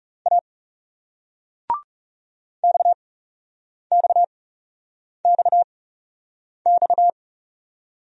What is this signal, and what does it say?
Morse code, heard as a keyed tone, about 40 wpm, 700 Hz. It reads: A X X X X